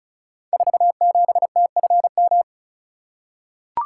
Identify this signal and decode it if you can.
Morse code, heard as a keyed tone, about 36 wpm, 700 Hz. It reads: 47TFM
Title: Code Groups - 5 Letter:Numbers - 35wpm